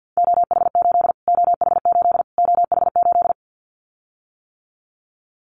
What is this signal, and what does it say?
Morse code, heard as a keyed tone, about 50 wpm, 700 Hz. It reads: OH8 OH8 OH8
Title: Callsign Prefixes for Top 25 DXCC - Repeated 3x - 50wpm